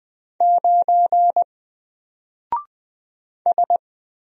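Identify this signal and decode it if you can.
Morse code, heard as a keyed tone, about 20 wpm, 700 Hz. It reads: 9 S
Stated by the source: Single letter and number Morse drill at 20 wpm, from N4LCD.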